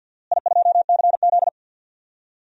Morse code, heard as a keyed tone, about 48 wpm, 700 Hz. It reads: I2CZ
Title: DXCC-100 Call Signs - 4 Characters - Repeated 2x - 50wpm